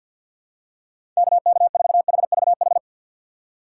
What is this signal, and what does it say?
Morse code, heard as a keyed tone, about 50 wpm, 700 Hz. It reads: KK4HVH
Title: USA Call Signs - 6 Characters - 4x Character Spacing - 50wpm